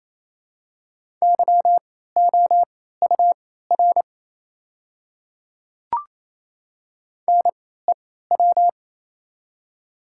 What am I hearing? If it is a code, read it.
Morse code, heard as a keyed tone, about 28 wpm, spaced as for 16 wpm, 700 Hz. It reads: YOUR NEW
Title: 3x Character Spacing - Top 300 Words - 28wpm